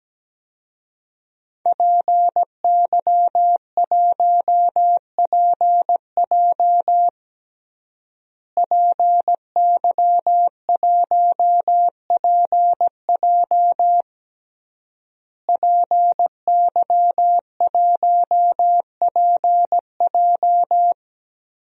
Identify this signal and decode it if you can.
Morse code, heard as a keyed tone, about 17 wpm, 700 Hz. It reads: PY1PJ PY1PJ PY1PJ
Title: DXCC-100 Call Signs - 5 Characters - Repeated 3x - 17wpm